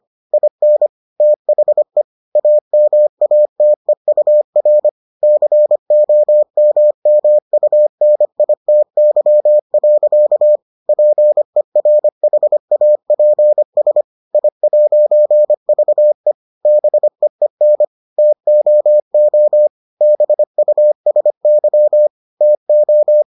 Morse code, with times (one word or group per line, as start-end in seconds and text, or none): 0.3-0.9 IN
1.2-2.0 THE
2.4-4.9 AMATEUR
5.2-10.6 COMMUNITY.
10.9-14.0 PERHAPS
14.3-16.3 I'VE
16.7-17.8 BEEN
18.2-19.7 TOO
20.0-22.1 BUSY
22.4-23.2 TO